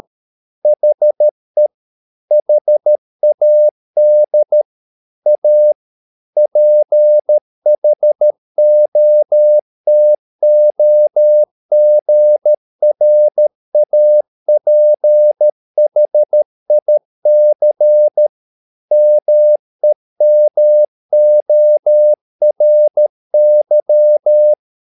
Morse code, heard as a keyed tone, about 13 wpm, 600 Hz. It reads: HE HAD A PHOTOGRAPHIC MEMORY
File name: mp3 files / Lexiphiles - 13wpm